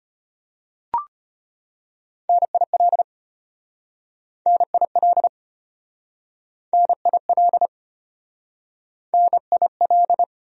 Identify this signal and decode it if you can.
Morse code, heard as a keyed tone, about 34 wpm, 700 Hz. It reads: NIL NIL NIL NIL